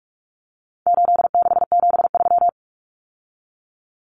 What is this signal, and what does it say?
Morse code, heard as a keyed tone, about 44 wpm, 700 Hz. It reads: 8673
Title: Common Cut Numbers - 4-Digit 45wpm